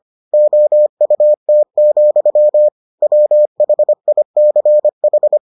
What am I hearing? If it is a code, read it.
Morse code, heard as a keyed tone, about 25 wpm, 600 Hz. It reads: OUT, WHICH